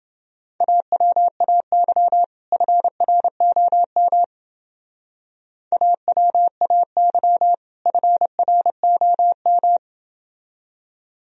Morse code, heard as a keyed tone, about 28 wpm, 700 Hz. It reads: AWAY FROM AWAY FROM